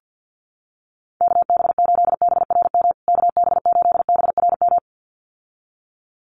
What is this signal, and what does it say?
Morse code, heard as a keyed tone, about 50 wpm, 700 Hz. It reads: KB8BRM KB8BRM
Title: USA Call Signs Repeated 2x 50wpm